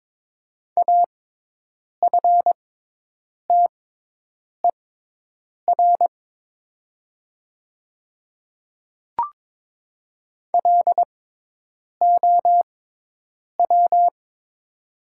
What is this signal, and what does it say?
Morse code, heard as a keyed tone, about 22 wpm, spaced as for 8 wpm, 700 Hz. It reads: AFTER LOW
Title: 6x Character Spacing - Top 200 Words - 22wpm